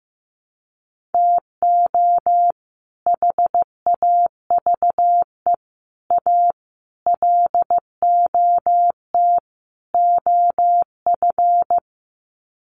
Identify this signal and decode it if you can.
Morse code, heard as a keyed tone, about 15 wpm, 700 Hz. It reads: TO HAVE A LOT OF
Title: Sets of 5 Words Encore 15wpm